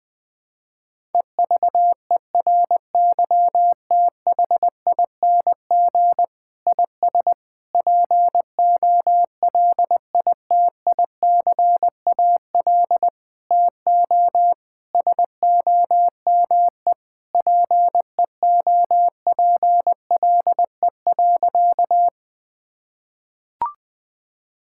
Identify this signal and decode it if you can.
Morse code, heard as a keyed tone, about 20 wpm, 700 Hz. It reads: EVERYTHING IS POLITICAL TO SOME PEOPLE.
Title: Sentences from Top 1000 Words - Review Part 5 - 20wpm